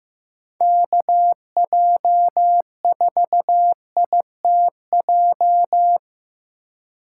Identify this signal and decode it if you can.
Morse code, heard as a keyed tone, about 18 wpm, 700 Hz. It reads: KJ4ITJ
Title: Speed Racing - USA 2x3 Call Signs 15wpm